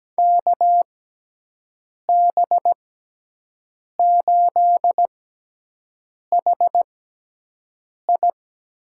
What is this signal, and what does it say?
Morse code, heard as a keyed tone, about 17 wpm, spaced as for 6 wpm, 700 Hz. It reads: KB8HI
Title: USA Call Signs - 6 Characters - 6x Character Spacing - 17wpm